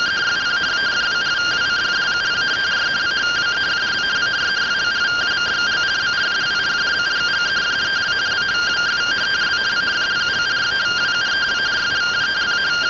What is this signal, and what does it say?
Two-tone teleprinter signal, about 170 Hz shift.